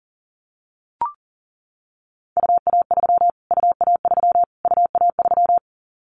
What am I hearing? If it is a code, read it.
Morse code, heard as a keyed tone, about 40 wpm, 700 Hz. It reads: UA3 UA3 UA3
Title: Callsign Prefixes for Top 25 DXCC - Repeated 3x - 40wpm